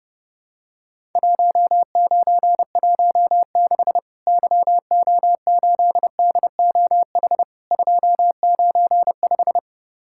Morse code, heard as a keyed tone, about 31 wpm, 700 Hz. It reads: 1916 YO8DOH 295
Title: Worked All Europe DX Contest - QTC Exchange - 30wpm